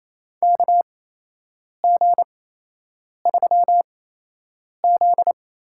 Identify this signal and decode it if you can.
Morse code, heard as a keyed tone, about 28 wpm, spaced as for 8 wpm, 700 Hz. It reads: KG3Z
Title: USA Call Signs - 4 Characters - 8x Character Spacing - 28wpm